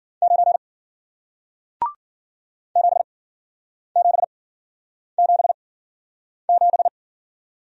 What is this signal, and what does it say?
Morse code, heard as a keyed tone, about 53 wpm, spaced as for 9 wpm, 700 Hz. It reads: 9 7777